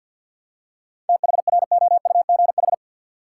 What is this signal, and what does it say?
Morse code, heard as a keyed tone, about 50 wpm, 700 Hz. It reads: THROUGH
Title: Seven Letter Words - 50wpm